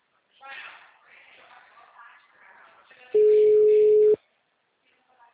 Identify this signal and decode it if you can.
Ringback tone (Europe, 425 Hz), with speech.
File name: out- -2005- - - .1522